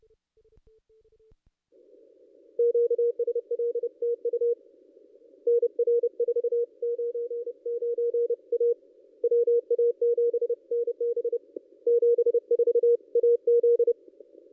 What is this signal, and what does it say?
Morse code, heard as a keyed tone, about 30 wpm, 470 Hz. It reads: QSL TU NR 499 A WA7NB 74 AZ
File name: contests / ss-cw-k5tr-op-n5rz -1800